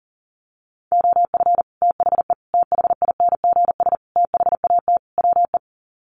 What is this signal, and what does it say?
Morse code, heard as a keyed tone, about 40 wpm, 700 Hz. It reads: OF THE THINGS THAT WE
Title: Sets of 5 Words Encore 40wpm